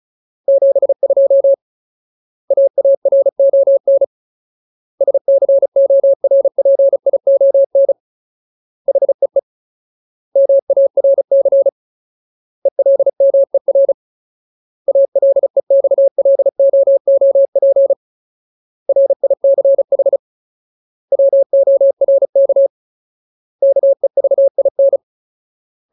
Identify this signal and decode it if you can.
Morse code, heard as a keyed tone, about 34 wpm, 550 Hz. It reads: 72 AARON SCORPION HEE MARC ELMER ALEXLOOP RICH WORK KEVIN
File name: Ultimate Word List / 35WPM Ultimate